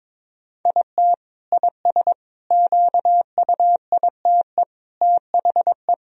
Morse code, heard as a keyed tone, about 22 wpm, 700 Hz. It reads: IT IS QUITE THE
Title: Sentences from Top 1000 Words - Review Part 2 - 22wpm